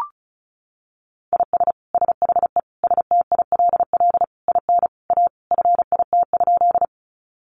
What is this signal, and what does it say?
Morse code, heard as a keyed tone, about 34 wpm, 700 Hz. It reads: IS SHE STILL IN A FIT?
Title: Sentences from Top 800 Words 35wpm